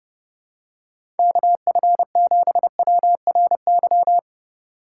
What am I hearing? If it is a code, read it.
Morse code, heard as a keyed tone, about 30 wpm, 700 Hz. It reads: KF7WRY
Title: USA Call Signs - 3x Character Spacing - 30wpm